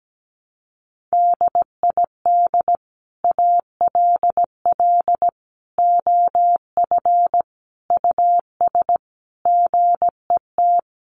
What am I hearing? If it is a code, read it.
Morse code, heard as a keyed tone, about 17 wpm, 700 Hz. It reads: DID ALL OF US GET
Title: A Sound Of Thunder 17wpm